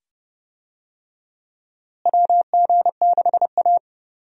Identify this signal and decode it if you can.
Morse code, heard as a keyed tone, about 30 wpm, 700 Hz. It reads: WG6A
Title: USA Call Signs - 4 Characters - Repeated 2x - 30wpm